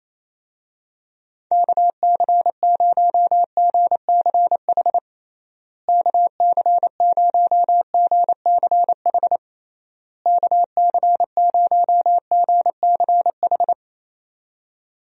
Morse code, heard as a keyed tone, about 28 wpm, 700 Hz. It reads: KC0GCH KC0GCH KC0GCH
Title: USA Call Signs - 6 Characters - Repeated 3x - 28wpm